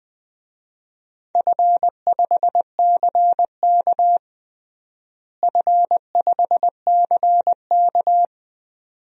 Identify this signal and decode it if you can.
Morse code, heard as a keyed tone, about 20 wpm, 700 Hz. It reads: F5CK F5CK
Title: DXCC-100 Call Signs - 4 Characters - Repeated 2x - 20wpm